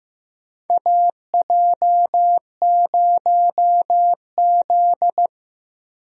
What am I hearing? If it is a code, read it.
Morse code, heard as a keyed tone, about 15 wpm, 700 Hz. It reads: AJ0Z
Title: USA Call Signs - 4 Characters - 15wpm